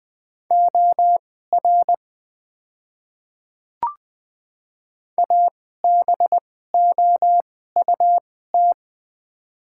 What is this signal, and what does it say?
Morse code, heard as a keyed tone, about 20 wpm, spaced as for 14 wpm, 700 Hz. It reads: OR ABOUT
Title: 2x Character Spacing - Top 100 Words - 20wpm